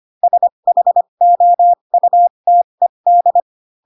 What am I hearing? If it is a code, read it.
Morse code, heard as a keyed tone, about 25 wpm, spaced as for 22 wpm, 700 Hz. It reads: SHOUTED
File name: Pinocchio / Pinocchio ch2 18 25 ik1hge